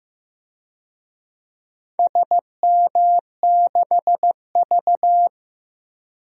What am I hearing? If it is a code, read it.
Morse code, heard as a keyed tone, about 15 wpm, 700 Hz. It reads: SM6V
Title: DXCC-100 Call Signs - 4 Characters - Repeated 3x - 15wpm